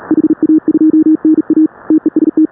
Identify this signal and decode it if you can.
Morse code, heard as a keyed tone, about 36 wpm, 310 Hz. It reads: HA2NA TEST
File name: AUDIO2019 / HA5BA CW